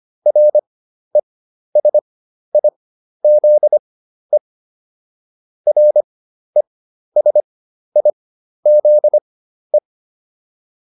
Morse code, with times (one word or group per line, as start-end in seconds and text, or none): 0.3-4.4 RESIZE
5.7-9.8 RESIZE